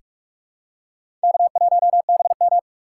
Morse code, heard as a keyed tone, about 45 wpm, 700 Hz. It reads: K1BM
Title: USA Call Signs - 4 Characters - 4x Character Spacing - 45wpm